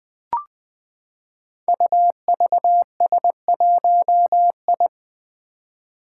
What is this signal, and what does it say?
Morse code, heard as a keyed tone, about 20 wpm, 700 Hz. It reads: UVS1I
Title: Five Challenging Letter-Number Combinations 20wpm